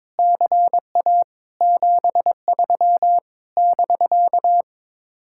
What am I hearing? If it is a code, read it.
Morse code, heard as a keyed tone, about 22 wpm, 700 Hz. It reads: CA 73 <BK>
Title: Speed-Racing Common QSO Phrases - 22wpm